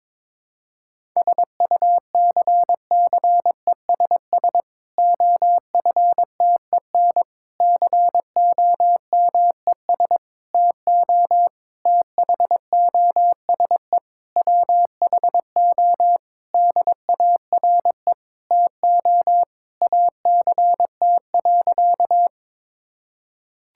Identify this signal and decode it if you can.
Morse code, heard as a keyed tone, about 22 wpm, 700 Hz. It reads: SUCCESS OFTEN COMES TO THOSE WHO DARE TO ACT.
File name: Difficult Sentences with Hints - 22wpm